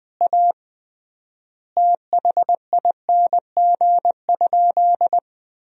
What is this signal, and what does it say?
Morse code, heard as a keyed tone, about 20 wpm, 700 Hz. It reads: A THING?
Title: 3x Word-Spacing - Sentences from Top 200 Words - 20wpm